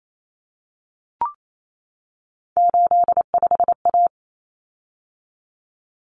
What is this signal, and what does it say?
Morse code, heard as a keyed tone, about 28 wpm, 700 Hz. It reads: 85A
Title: Common Cut Numbers - 3-Digit 28wpm